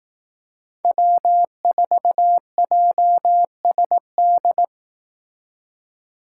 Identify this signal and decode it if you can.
Morse code, heard as a keyed tone, about 18 wpm, 700 Hz. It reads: W4JSD
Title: Speed Racing - USA 1x3 Call Signs 15wpm